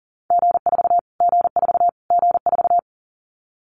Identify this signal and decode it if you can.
Morse code, heard as a keyed tone, about 40 wpm, 700 Hz. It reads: G4 G4 G4
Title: Callsign Prefixes for Top 25 DXCC - Repeated 3x - 40wpm